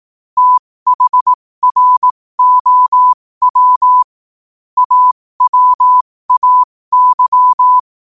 Morse code, heard as a keyed tone, about 18 wpm, spaced as for 15 wpm, 1000 Hz. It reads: THROW AWAY